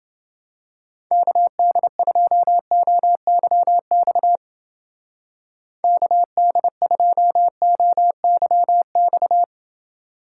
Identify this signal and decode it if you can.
Morse code, heard as a keyed tone, about 28 wpm, 700 Hz. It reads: KD2OYX KD2OYX